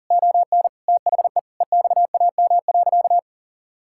Morse code, heard as a keyed tone, about 40 wpm, 700 Hz. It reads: ON THE EXAM.
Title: Sentences from Top 1000 Words - Review Part 5 - 40wpm